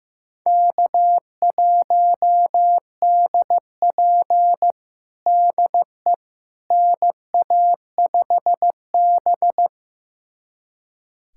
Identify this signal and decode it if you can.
Morse code, heard as a keyed tone, about 15 wpm, 700 Hz. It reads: K1DP DE NA5B
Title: USA Callsign DE Callsign - 4 Characters - 15wpm